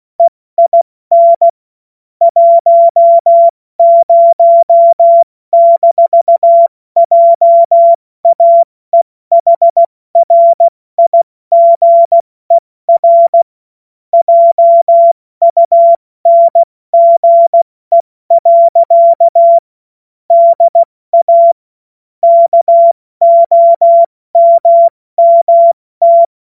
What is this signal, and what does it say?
Morse code, heard as a keyed tone, about 16 wpm, 680 Hz. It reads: EIN 10-JAEHRIGER JUNGE. DA KOMMT
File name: Morsekurs / Witze / Witz 16. Februar 2021, 70 F80